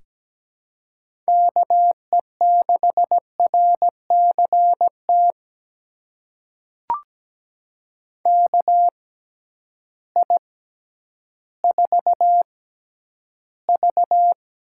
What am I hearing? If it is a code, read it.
Morse code, heard as a keyed tone, about 17 wpm, spaced as for 10 wpm, 700 Hz. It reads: KE6RCT KI4V
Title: USA Call Signs - 6 Characters - 6x Character Spacing - 17wpm